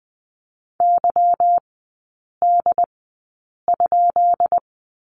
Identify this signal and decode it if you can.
Morse code, heard as a keyed tone, about 20 wpm, spaced as for 8 wpm, 700 Hz. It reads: YD?
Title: 205 Lesson 33e - Letter j - Three Characters 20wpm